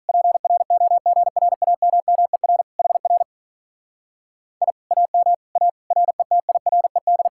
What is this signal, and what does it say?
Morse code, heard as a keyed tone, about 46 wpm, 700 Hz. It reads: PROGRAMMER HR I AM A RETIRED